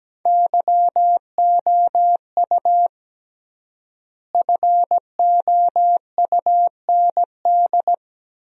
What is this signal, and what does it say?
Morse code, heard as a keyed tone, about 17 wpm, 700 Hz. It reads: YOU FOUND
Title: 3x Word-Spacing - Sentences from Top 300 Words - 17wpm